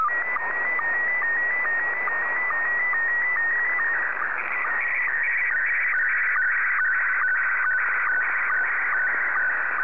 SSTV (slow-scan television), Scottie 1.